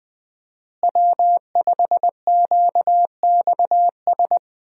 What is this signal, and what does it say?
Morse code, heard as a keyed tone, about 21 wpm, 700 Hz. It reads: W5QXS
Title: USA Call Signs - 5 Characters - Repeated 3x - 20wpm